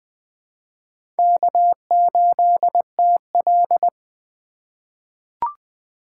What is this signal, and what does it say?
Morse code, heard as a keyed tone, about 21 wpm, 700 Hz. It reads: K8TL
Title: USA Call Signs - 4 Characters - 2x Character Spacing - 20wpm